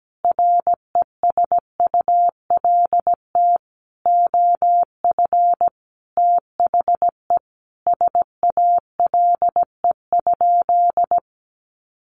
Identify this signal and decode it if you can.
Morse code, heard as a keyed tone, about 17 wpm, 700 Hz. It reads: RESULT OF THE SALE?
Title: Sentences from Top 1400 Words 17wpm